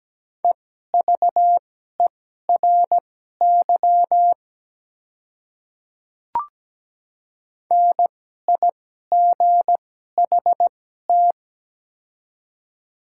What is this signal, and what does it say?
Morse code, heard as a keyed tone, about 17 wpm, spaced as for 12 wpm, 700 Hz. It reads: EVERY NIGHT
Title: 2x Character Spacing - Top 300 Words - 17wpm